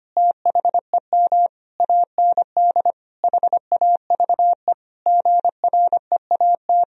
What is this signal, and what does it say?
Morse code, heard as a keyed tone, about 25 wpm, 700 Hz. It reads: THEM AND HAVE GREAT
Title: Sentences from Top 1000 Words - Review Part 4 - 25wpm